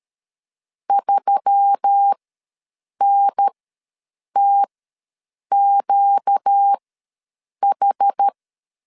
Morse code, heard as a keyed tone, about 13 wpm, spaced as for 7 wpm, 800 Hz. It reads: 3NTQH
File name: LEZIONI / 5 LBVC / 5 L ws 12wpm 2958